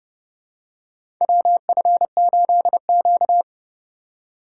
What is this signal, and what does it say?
Morse code, heard as a keyed tone, about 32 wpm, 700 Hz. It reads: WF8Q
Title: USA Call Signs - 4 Characters - 2x Character Spacing - 30wpm